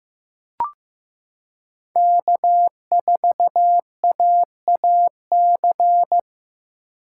Morse code, heard as a keyed tone, about 16 wpm, 700 Hz. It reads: K4AAC
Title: K1USN SST QSO Practice - 15wpm